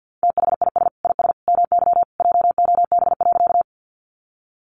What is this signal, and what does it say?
Morse code, heard as a keyed tone, about 48 wpm, 700 Hz. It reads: THIS IS MY JOB.
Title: Sentences from Top 600 Words 50wpm